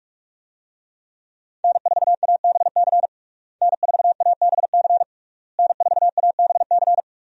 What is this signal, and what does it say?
Morse code, heard as a keyed tone, about 41 wpm, 700 Hz. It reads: N4ABC N4ABC N4ABC
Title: USA Call Signs - 5 Characters - Repeated 3x - 45wpm